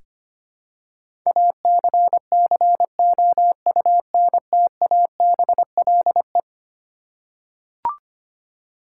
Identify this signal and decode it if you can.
Morse code, heard as a keyed tone, about 25 wpm, 700 Hz. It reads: ACCOUNTABLE
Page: morse-code-ninja-practice-25wpm